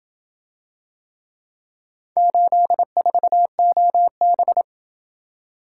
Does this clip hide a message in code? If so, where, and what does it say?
Yes: Morse code from 2.2 to 4.6 s, heard as a keyed tone, about 27 wpm, 700 Hz. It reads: 84OB